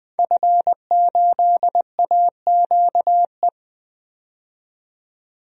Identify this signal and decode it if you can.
Morse code, heard as a keyed tone, about 20 wpm, 700 Hz. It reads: F8AQE
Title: DXCC-100 Call Signs - 5 Characters - 20wpm